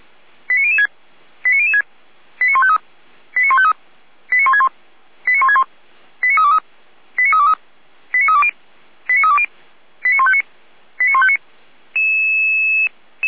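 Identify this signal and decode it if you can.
Five-tone selective call (ZVEI), heard as quick runs of five tones, tones of about 80 ms. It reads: 89006 89006 89153 89153 89161 89161 89323 89323 89320 89320 89159 89159
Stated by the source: ZVEI five-tone call received by SDR, from audio.f85360.feuerwehr-orlamuende.org fire brigade server.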